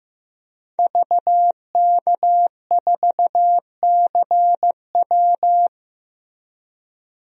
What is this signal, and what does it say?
Morse code, heard as a keyed tone, about 15 wpm, 700 Hz. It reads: VK4CW
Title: DXCC-100 Call Signs - 5 Characters - Repeated 2x - 15wpm